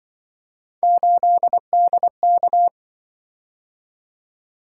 Morse code, heard as a keyed tone, about 25 wpm, 700 Hz. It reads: 8DK